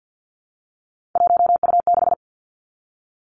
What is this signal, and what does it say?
Morse code, heard as a keyed tone, about 50 wpm, 700 Hz. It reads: 1U6
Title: Three Challenging Letter-Number Combinations 50wpm